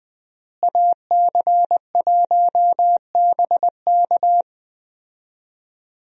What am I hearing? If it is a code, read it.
Morse code, heard as a keyed tone, about 20 wpm, 700 Hz. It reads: AC1BK
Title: USA Call Signs - 5 Characters - Repeated 3x - 20wpm